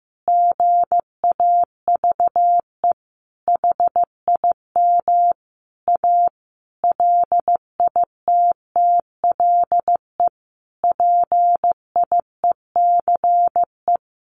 Morse code, heard as a keyed tone, about 15 wpm, 700 Hz. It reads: GAVE HIM A LITTLE PIECE
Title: Sentences from Top 1500 Words 15wpm